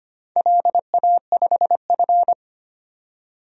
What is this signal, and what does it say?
Morse code, heard as a keyed tone, about 25 wpm, 690 Hz. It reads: LA5F
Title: DXCC-100 Call Signs - 4 Characters - 25wpm